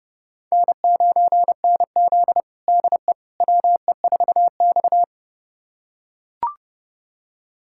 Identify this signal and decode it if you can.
Morse code, heard as a keyed tone, about 30 wpm, 700 Hz. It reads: N9NZ DE WE4X
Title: USA Callsign DE Callsign - 4 Characters - 30wpm